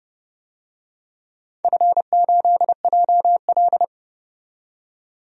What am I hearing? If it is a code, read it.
Morse code, heard as a keyed tone, about 29 wpm, 700 Hz. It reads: F8JL
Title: DXCC-100 Call Signs - 4 Characters - 30wpm